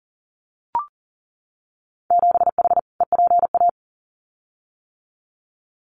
Morse code, heard as a keyed tone, about 43 wpm, 700 Hz. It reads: 7H EPA
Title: Winter Field Day Exchanges Encore 40wpm